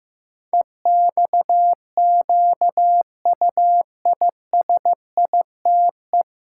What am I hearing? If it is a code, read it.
Morse code, heard as a keyed tone, about 15 wpm, 700 Hz. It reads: EXQUISITE
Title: Pangram Challenge v2 - 15wpm